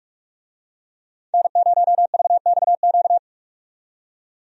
Morse code, heard as a keyed tone, about 46 wpm, 700 Hz. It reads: N0VXQ
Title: USA Call Signs - 5 Characters - 45wpm